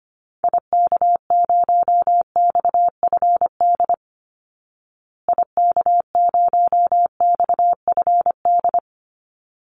Morse code, heard as a keyed tone, about 25 wpm, 700 Hz. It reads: IK0XFD IK0XFD
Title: Call Signs Repeated 3x - 25wpm